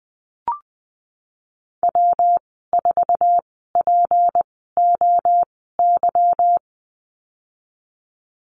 Morse code, heard as a keyed tone, about 20 wpm, spaced as for 14 wpm, 700 Hz. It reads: W4POY
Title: 154 Lesson 25g - Letter p - Call Signs 20wpm